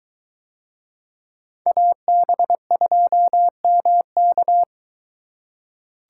Morse code, heard as a keyed tone, about 23 wpm, 700 Hz. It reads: AB2MK